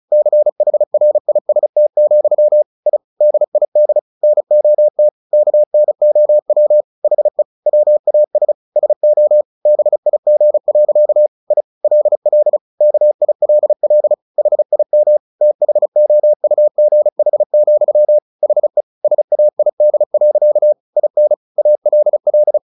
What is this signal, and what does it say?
Morse code, heard as a keyed tone, about 34 wpm, 600 Hz. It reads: CHRIST, I DID NOT KNOW HE WAS SO BIG. I LL KILL HIM THOUGH, HE SAID. IN ALL
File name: the old man and the sea / the old man and the sea 1 35wpm